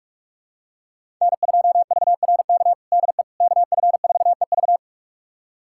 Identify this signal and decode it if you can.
Morse code, heard as a keyed tone, about 46 wpm, 700 Hz. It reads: N2VRK DE KF4EV
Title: USA Callsign DE Callsign - 5 Characters - 45wpm